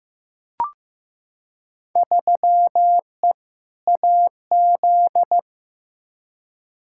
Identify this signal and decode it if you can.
Morse code, heard as a keyed tone, about 15 wpm, 700 Hz. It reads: 3E AZ
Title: Field Day Exchanges - 15wpm